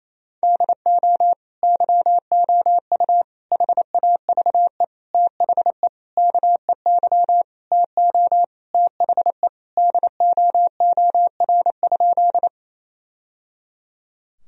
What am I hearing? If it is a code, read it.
Morse code, heard as a keyed tone, about 28 wpm, 700 Hz. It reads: DO YOU HAVE THE KEY TO THE DOOR?
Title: Sentences from Top 1000 Words - Review Part 4 - 28wpm